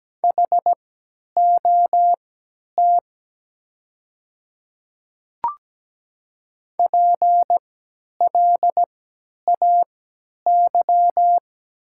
Morse code, heard as a keyed tone, about 17 wpm, spaced as for 10 wpm, 700 Hz. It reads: HOT PLAY
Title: 3x Character Spacing - Top 200 Words - 17wpm